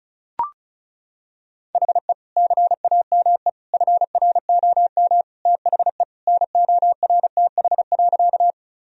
Morse code, heard as a keyed tone, about 36 wpm, 700 Hz. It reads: HE CAME FROM THE NORTH.
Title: Sentences from Top 1000 Words - Review Part 1 - 35wpm